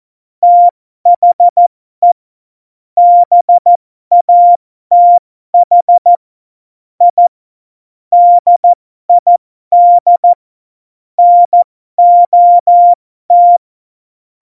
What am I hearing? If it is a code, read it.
Morse code, heard as a keyed tone, about 14 wpm, spaced as for 12 wpm, 700 Hz. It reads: THE BATH I DID NOT
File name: Morsekurs / Witze